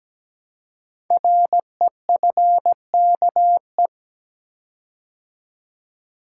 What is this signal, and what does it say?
Morse code, heard as a keyed tone, about 17 wpm, 700 Hz. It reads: REFKE
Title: Code Groups - 5 Letter:Numbers - 17wpm